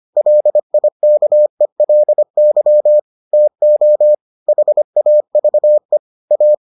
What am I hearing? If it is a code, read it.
Morse code, heard as a keyed tone, about 25 wpm, 600 Hz. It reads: LIKELY TO HAVE A